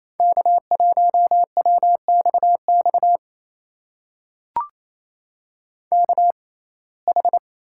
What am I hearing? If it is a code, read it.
Morse code, heard as a keyed tone, about 28 wpm, spaced as for 20 wpm, 700 Hz. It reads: K1WXX KH
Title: USA Call Signs - 5 Characters - 6x Character Spacing - 28wpm